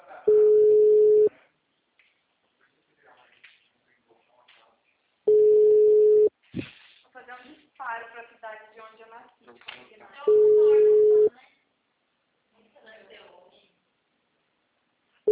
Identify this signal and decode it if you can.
Ringback tone (Europe, 425 Hz), with speech.